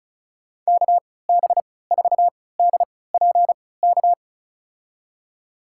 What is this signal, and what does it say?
Morse code, heard as a keyed tone, about 34 wpm, spaced as for 20 wpm, 700 Hz. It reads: KB4DPK
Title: USA Call Signs - 6 Characters - 3x Character Spacing - 35wpm